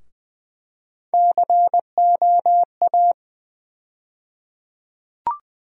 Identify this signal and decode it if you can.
Morse code, heard as a keyed tone, about 20 wpm, 700 Hz. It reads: COA
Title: USA Call Sign Postfixes - 3 Letters 20wpm